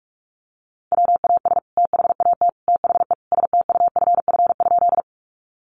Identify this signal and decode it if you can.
Morse code, heard as a keyed tone, about 44 wpm, 700 Hz. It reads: WAS THAT THE STUFF?
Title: Sentences from Top 1400 Words 45wpm